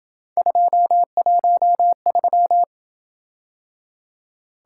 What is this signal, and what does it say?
Morse code, heard as a keyed tone, about 27 wpm, 700 Hz. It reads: 213